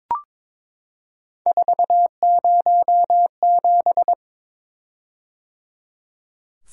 Morse code, heard as a keyed tone, about 22 wpm, 700 Hz. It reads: 407
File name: practice / 22wpm / Three Letter-Number Combinations - 22wpm